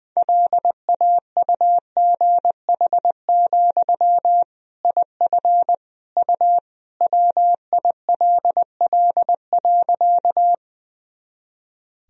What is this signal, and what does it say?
Morse code, heard as a keyed tone, about 20 wpm, 700 Hz. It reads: LAUGH, IF U WILL.
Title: Sentences from Top 500 Words and Abbreviations 20wpm